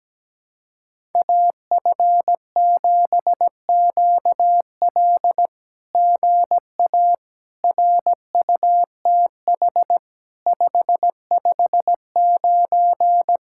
Morse code, heard as a keyed tone, about 17 wpm, 700 Hz. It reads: AF7QL GA RUTH 559
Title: SOTA QSO Practice 17wpm